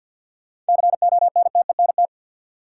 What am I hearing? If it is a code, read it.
Morse code, heard as a keyed tone, about 53 wpm, 700 Hz. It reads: CONTENT
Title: Sentences from Top 900 Words - New Word Drill - 50wpm